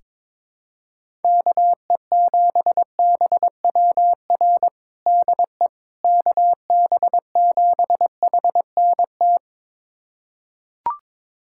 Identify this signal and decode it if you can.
Morse code, heard as a keyed tone, about 22 wpm, 700 Hz. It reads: KE7BWR DE KB7HNT
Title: USA Callsign DE Callsign - 6 Characters - 22wpm